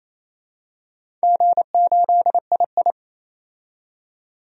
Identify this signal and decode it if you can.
Morse code, heard as a keyed tone, about 27 wpm, 690 Hz. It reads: G8II
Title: DXCC-100 Call Signs - 4 Characters - 28wpm